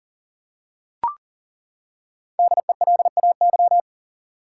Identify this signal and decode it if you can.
Morse code, heard as a keyed tone, about 40 wpm, 700 Hz. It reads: DELAY
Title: Five Letter Words - 40wpm